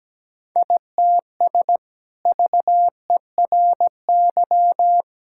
Morse code, heard as a keyed tone, about 17 wpm, 700 Hz. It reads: ITS VERY
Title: Sentences from Top 1000 Words - Review Part 5 - 17wpm